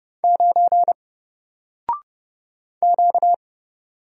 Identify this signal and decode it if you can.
Morse code, heard as a keyed tone, about 30 wpm, 700 Hz. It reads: 9 Q
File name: Single-Letter-Number-30wpm